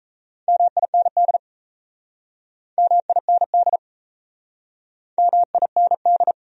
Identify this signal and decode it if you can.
Morse code, heard as a keyed tone, about 37 wpm, 700 Hz. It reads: MIND MIND MIND